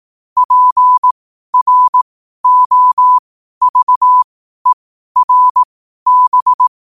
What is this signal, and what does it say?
Morse code, heard as a keyed tone, about 18 wpm, spaced as for 12 wpm, 1000 Hz. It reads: PROVERB
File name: qotdfiles / 12wpm